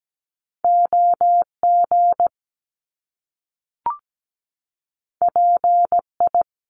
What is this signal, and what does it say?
Morse code, heard as a keyed tone, about 17 wpm, 700 Hz. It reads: OG PI
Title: Top Two Letter Combinations in Words 17wpm